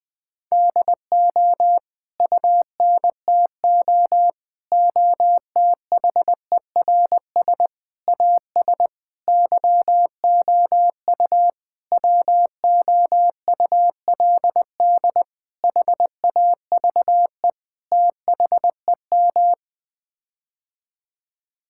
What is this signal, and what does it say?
Morse code, heard as a keyed tone, about 21 wpm, 700 Hz. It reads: DO UNTO OTHERS AS YOU WOULD HAVE THEM
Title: Sets of 8 Words - 20wpm